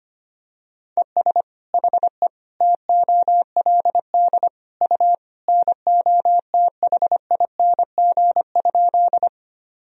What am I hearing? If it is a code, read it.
Morse code, heard as a keyed tone, about 25 wpm, 700 Hz. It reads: ES HE TOLD U NOTHING?
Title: Sentences from Top 500 Words and Abbreviations 25wpm